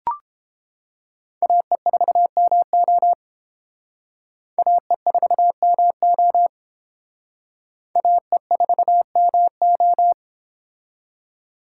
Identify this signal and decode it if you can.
Morse code, heard as a keyed tone, about 30 wpm, 700 Hz. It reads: AE4MO AE4MO AE4MO